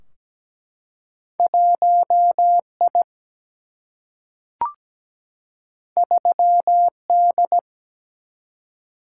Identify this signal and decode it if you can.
Morse code, heard as a keyed tone, about 17 wpm, 700 Hz. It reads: 1I 3D
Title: Single-Letter Single-Number 17wpm